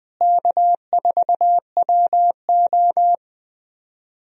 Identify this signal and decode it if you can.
Morse code, heard as a keyed tone, about 20 wpm, 700 Hz. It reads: K4WO
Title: USA Call Signs - 4 Characters - Repeated 2x - 20wpm